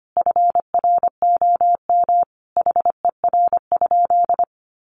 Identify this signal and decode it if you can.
Morse code, heard as a keyed tone, about 25 wpm, 700 Hz. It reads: FROM HER?
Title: Sentences from Top 1500 Words 25wpm